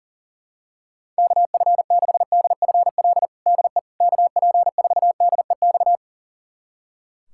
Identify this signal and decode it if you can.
Morse code, heard as a keyed tone, about 40 wpm, 700 Hz. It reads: KF6DFL DE KP4DEX
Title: USA Callsign DE Callsign - 6 Characters - 40wpm